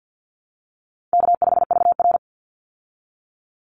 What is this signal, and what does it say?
Morse code, heard as a keyed tone, about 50 wpm, 700 Hz. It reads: K5VR
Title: Call Signs Repeated 3x - 50wpm